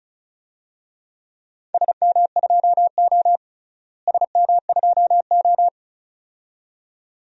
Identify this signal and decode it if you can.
Morse code, heard as a keyed tone, about 36 wpm, 700 Hz. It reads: SM2O SM2O
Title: DXCC-100 Call Signs - 4 Characters - Repeated 2x - 35wpm